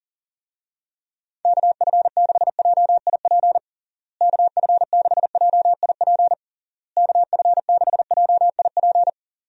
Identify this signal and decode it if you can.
Morse code, heard as a keyed tone, about 40 wpm, 700 Hz. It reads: KF6JIP KF6JIP KF6JIP
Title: USA Call Signs - 6 Characters - Repeated 3x - 40wpm